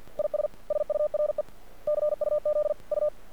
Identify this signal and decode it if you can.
Morse code, heard as a keyed tone, about 50 wpm, 600 Hz. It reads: IS SURE CUZ U
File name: archived / CRSnet